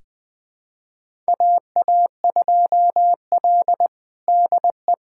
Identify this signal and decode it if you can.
Morse code, heard as a keyed tone, about 20 wpm, 700 Hz. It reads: AA2L DE
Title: USA Callsign DE Callsign - 4 Characters - 20wpm